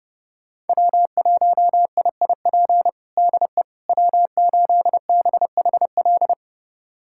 Morse code, heard as a keyed tone, about 30 wpm, 700 Hz. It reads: W1IIP DE W8BHL
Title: USA Callsign DE Callsign - 5 Characters - 30wpm